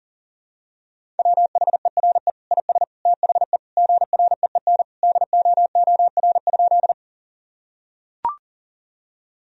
Morse code, heard as a keyed tone, about 40 wpm, 700 Hz. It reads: WHERE IS THE GREEN DOOR?
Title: Sentences from Top 1000 Words - Review Part 2 - 40wpm